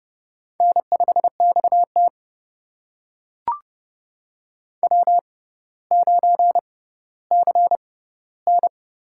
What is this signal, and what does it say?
Morse code, heard as a keyed tone, about 30 wpm, spaced as for 15 wpm, 700 Hz. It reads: N5XT W9CN
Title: USA Call Signs - 4 Characters - 6x Character Spacing - 30wpm